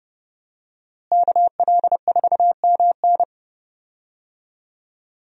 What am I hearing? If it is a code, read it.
Morse code, heard as a keyed tone, about 30 wpm, 700 Hz. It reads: KL4MN
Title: USA Call Signs - 5 Characters - Repeated 2x - 30wpm